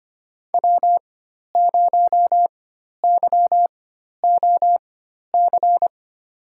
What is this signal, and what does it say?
Morse code, heard as a keyed tone, about 25 wpm, spaced as for 12 wpm, 700 Hz. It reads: W0YOC
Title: USA Call Signs - 5 Characters - 4x Character Spacing - 25wpm